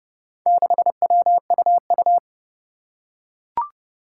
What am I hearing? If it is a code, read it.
Morse code, heard as a keyed tone, about 30 wpm, 700 Hz. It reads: 6WUU
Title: Four Challenging Letter-Number Combinations 30wpm